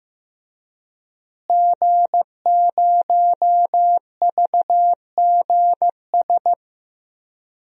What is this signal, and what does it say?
Morse code, heard as a keyed tone, about 15 wpm, 700 Hz. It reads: G0VGS
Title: DXCC-100 Call Signs - 5 Characters - Repeated 3x - 15wpm